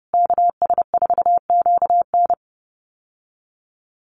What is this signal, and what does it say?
Morse code, heard as a keyed tone, about 30 wpm, 700 Hz. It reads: KS4QN
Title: USA Call Signs Repeated 3x 30wpm